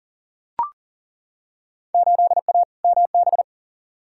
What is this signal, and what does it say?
Morse code, heard as a keyed tone, about 40 wpm, 700 Hz. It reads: 8A MB
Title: Field Day Exchanges Encore - 40wpm